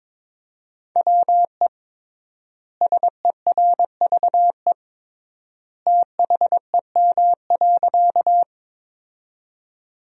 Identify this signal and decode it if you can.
Morse code, heard as a keyed tone, about 22 wpm, 700 Hz. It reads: WE SERVE THEM.
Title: 3x Word-Spacing - Sentences from Top 400 Words - 22wpm